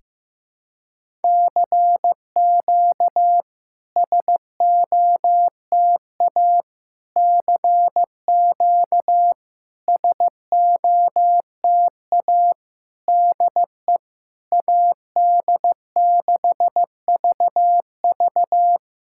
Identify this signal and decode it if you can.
Morse code, heard as a keyed tone, about 15 wpm, 700 Hz. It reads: CQ SOTA CQ SOTA DE AD6VV
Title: SOTA QSO Practice 15wpm